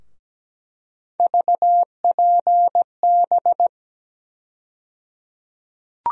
Morse code, heard as a keyed tone, about 17 wpm, 700 Hz. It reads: VPB
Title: USA Call Sign Postfixes - 3 Letters 17wpm